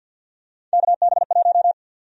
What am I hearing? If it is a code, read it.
Morse code, heard as a keyed tone, about 46 wpm, 700 Hz. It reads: KB1
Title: USA Call Sign Prefixes - 50wpm